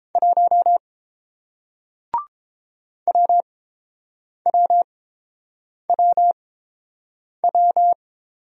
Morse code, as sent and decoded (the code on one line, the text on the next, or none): .---- / .-- .-- .-- .--
1 WWWW